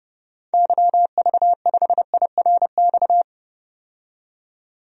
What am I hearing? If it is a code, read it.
Morse code, heard as a keyed tone, about 30 wpm, 700 Hz. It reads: YV5IRX
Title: DXCC-100 Call Signs - 6 Characters - 30wpm